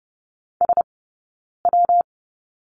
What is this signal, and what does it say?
Morse code, heard as a keyed tone, about 30 wpm, spaced as for 9 wpm, 690 Hz. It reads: SW
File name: Lesson 19d - Letter w - Two Characters 30wpm